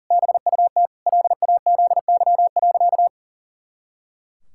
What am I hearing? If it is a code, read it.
Morse code, heard as a keyed tone, about 37 wpm, 700 Hz. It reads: BUT LAZY.
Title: Pangram Challenge v2 - 40wpm